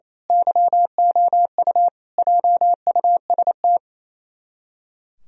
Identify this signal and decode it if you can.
Morse code, heard as a keyed tone, about 27 wpm, 700 Hz. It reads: YOU JUST
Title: Speed-Racing - Sets of 5 Words 28wpm